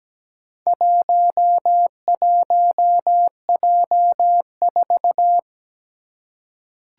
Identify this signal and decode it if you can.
Morse code, heard as a keyed tone, about 17 wpm, 700 Hz. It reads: 11J4
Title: Four Challenging Letter-Number Combinations 17wpm